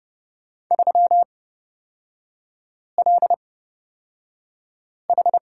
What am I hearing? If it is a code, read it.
Morse code, heard as a keyed tone, about 30 wpm, spaced as for 5 wpm, 700 Hz. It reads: 3LH
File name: Lesson 37e - Letter x - Three Characters 30wpm